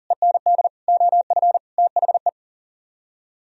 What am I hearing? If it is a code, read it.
Morse code, heard as a keyed tone, about 51 wpm, 700 Hz. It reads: END OF THE